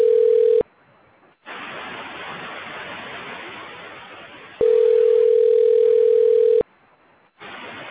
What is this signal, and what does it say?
Ringback tone (North America), 440 + 480 Hz.